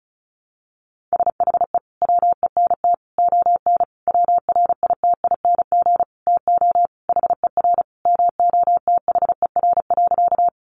Morse code, heard as a keyed tone, about 34 wpm, 700 Hz. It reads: SHE WENT ON WRITING TO HER MOTHER.
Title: Sentences from Top 800 Words 35wpm